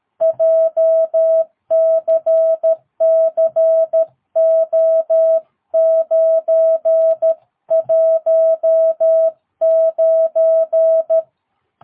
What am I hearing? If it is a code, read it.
Morse code, heard as a keyed tone, about 13 wpm, 650 Hz. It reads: JCCO919
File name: jh8cbh / 2022cw